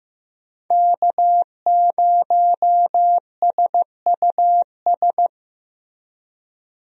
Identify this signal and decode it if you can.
Morse code, heard as a keyed tone, about 15 wpm, 700 Hz. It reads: K0SUS
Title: USA Call Signs - 5 Characters - Repeated 2x - 15wpm